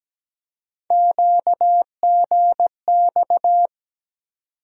Morse code, heard as a keyed tone, about 17 wpm, 700 Hz. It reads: QGX
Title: USA Call Sign Postfixes - 3 Letters 17wpm